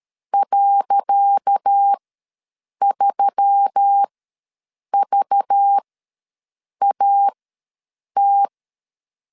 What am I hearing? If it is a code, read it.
Morse code, heard as a keyed tone, about 12 wpm, spaced as for 7 wpm, 800 Hz. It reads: .3VAT
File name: LEZIONI / 10 SIGNS / 10C DOT ws 12wpm 3548